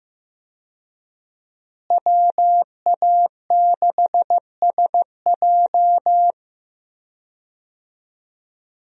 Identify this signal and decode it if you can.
Morse code, heard as a keyed tone, about 15 wpm, 700 Hz. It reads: WA6SJ
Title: USA Call Signs - 5 Characters - Repeated 2x - 15wpm